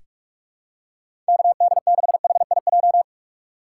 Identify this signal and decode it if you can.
Morse code, heard as a keyed tone, about 41 wpm, 700 Hz. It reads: KD6HIJ
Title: USA Call Signs - 6 Characters - 3x Character Spacing - 45wpm